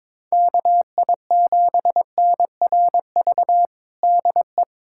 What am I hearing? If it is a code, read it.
Morse code, heard as a keyed tone, about 22 wpm, 700 Hz. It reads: KI7NRV DE
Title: USA Callsign DE Callsign 22wpm